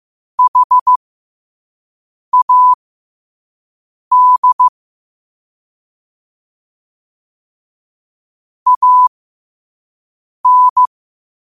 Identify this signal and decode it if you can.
Morse code, heard as a keyed tone, about 15 wpm, spaced as for 5 wpm, 1000 Hz. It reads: HAD AN
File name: 5wpm